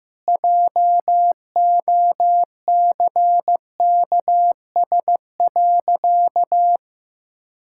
Morse code, heard as a keyed tone, about 16 wpm, 700 Hz. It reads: JOCKS.
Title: Pangram Challenge v2 - 15wpm